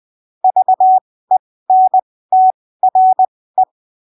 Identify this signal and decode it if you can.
Morse code, heard as a keyed tone, about 20 wpm, spaced as for 15 wpm, 750 Hz. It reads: VENTRE
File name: arquivos / texto / texto 27 20ppm